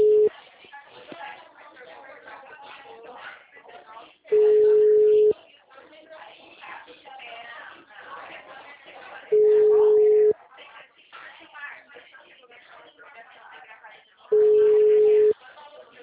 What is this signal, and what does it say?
Ringback tone (Europe, 425 Hz), with speech.
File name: out- -2002- - - .20586